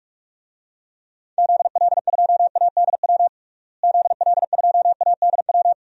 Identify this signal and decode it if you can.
Morse code, heard as a keyed tone, about 47 wpm, 700 Hz. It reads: ZL2ADW ZL2ADW
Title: DXCC-100 Call Signs - 6 Characters - Repeated 3x - 45wpm